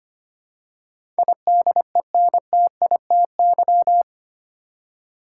Morse code, heard as a keyed tone, about 25 wpm, 700 Hz. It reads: IDENTITY
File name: Eight Letter Words - 25wpm